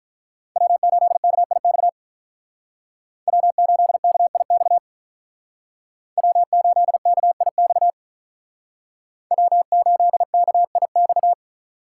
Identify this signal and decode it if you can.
Morse code, heard as a keyed tone, about 45 wpm, 700 Hz. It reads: W8KIX W8KIX W8KIX W8KIX